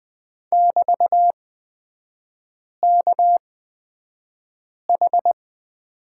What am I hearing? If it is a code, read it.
Morse code, heard as a keyed tone, about 20 wpm, spaced as for 5 wpm, 700 Hz. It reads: <BT>KH
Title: 229 Lesson 37e - Letter x - Three Characters 20wpm